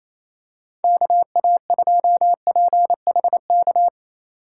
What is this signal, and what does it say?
Morse code, heard as a keyed tone, about 28 wpm, 700 Hz. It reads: KA2PHK
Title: USA Call Signs - 6 Characters - 8x Character Spacing - 28wpm